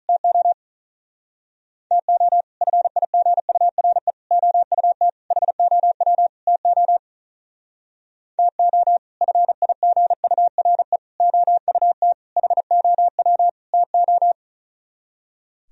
Morse code, heard as a keyed tone, about 45 wpm, 700 Hz. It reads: TO TO FIGURE OUT HOW TO TO FIGURE OUT HOW TO